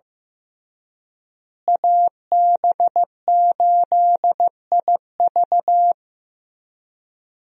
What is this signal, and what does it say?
Morse code, heard as a keyed tone, about 15 wpm, 700 Hz. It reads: AB8IV
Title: USA Call Signs - 5 Characters - Repeated 3x - 15wpm